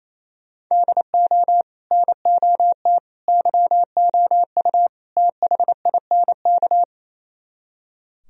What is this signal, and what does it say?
Morse code, heard as a keyed tone, about 36 wpm, 700 Hz. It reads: DO NOT YOU THINK